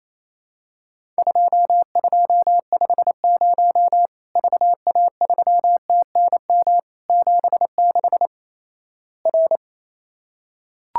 Morse code, heard as a keyed tone, about 28 wpm, 700 Hz. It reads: 2250 VA3TNM 76 R
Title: Worked All Europe DX Contest - QTC Exchange - 28wpm